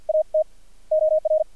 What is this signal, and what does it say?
Morse code, heard as a keyed tone, about 53 wpm, spaced as for 46 wpm, 620 Hz. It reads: AT OR
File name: archived / CRSnet